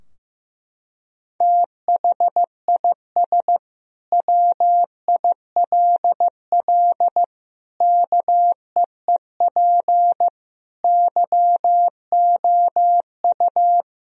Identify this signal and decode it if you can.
Morse code, heard as a keyed tone, about 15 wpm, 700 Hz. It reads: THIS WILL KEEP YOU
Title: Sentences from Top 1000 Words - Review Part 2 - 15wpm